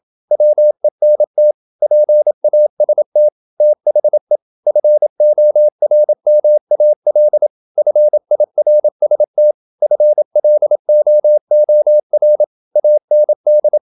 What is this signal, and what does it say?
Morse code, heard as a keyed tone, about 27 wpm, 600 Hz. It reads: WENT PAST THE FORMAL FIRST FLOOR AND
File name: Never 0027wpm